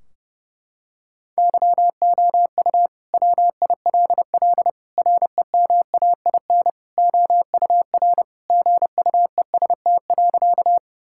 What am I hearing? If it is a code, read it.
Morse code, heard as a keyed tone, about 30 wpm, 700 Hz. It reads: YOU WILL REMAIN OUR GUEST.
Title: Sentences from Top 1600 Words 30wpm